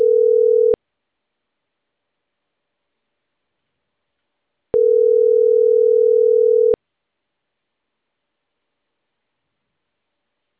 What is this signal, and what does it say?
Ringback tone (North America), 440 + 480 Hz.